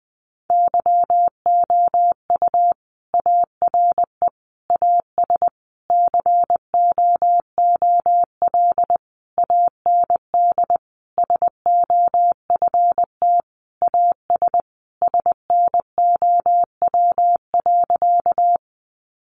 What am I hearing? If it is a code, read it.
Morse code, heard as a keyed tone, about 20 wpm, 700 Hz. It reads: YOU ARE AS COOL AND SOFT AS SNOW.
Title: Sentences from Top 1500 Words 20wpm